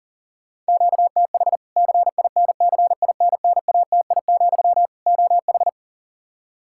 Morse code, heard as a keyed tone, about 40 wpm, 700 Hz. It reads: QTH CINCINNATI, OH
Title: Practice QSO - Exchange 1 - Beginning - 40wpm